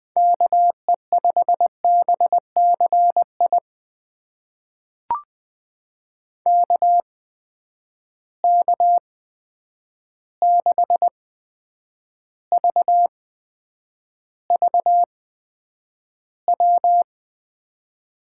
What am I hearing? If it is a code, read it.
Morse code, heard as a keyed tone, about 20 wpm, spaced as for 9 wpm, 700 Hz. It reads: KE5BCI KK6VVW
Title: USA Call Signs - 6 Characters - 8x Character Spacing - 20wpm